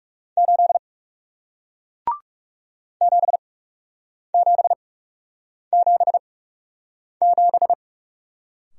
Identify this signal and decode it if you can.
Morse code, heard as a keyed tone, about 41 wpm, spaced as for 8 wpm, 700 Hz. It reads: 8 7777